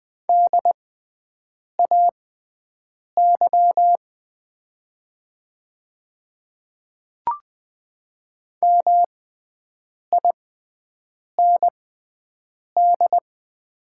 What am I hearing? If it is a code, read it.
Morse code, heard as a keyed tone, about 20 wpm, spaced as for 7 wpm, 700 Hz. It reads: DAY MIND
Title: 6x Character Spacing - Top 500 Words - 20wpm